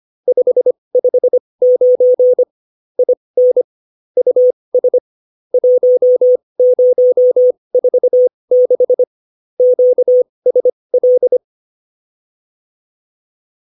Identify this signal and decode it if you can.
Morse code, heard as a keyed tone, about 25 wpm, spaced as for 20 wpm, 500 Hz. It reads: 559 IN US 1046 QSL